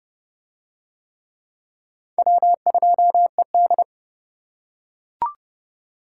Morse code, heard as a keyed tone, about 29 wpm, 700 Hz. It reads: W2ED